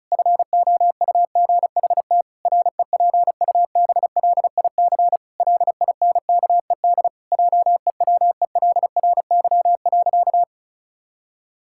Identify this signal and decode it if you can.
Morse code, heard as a keyed tone, about 34 wpm, 700 Hz. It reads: FOUGHT REPUBLIC LINKED JEWELRY.
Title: Pangram Challenge v2 - 35wpm